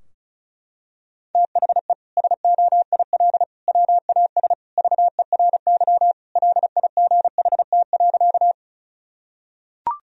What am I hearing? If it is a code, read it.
Morse code, heard as a keyed tone, about 36 wpm, 700 Hz. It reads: THE SOIL WAS VERY LIGHT.
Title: Sentences from Top 1000 Words - Review Part 4 - 35wpm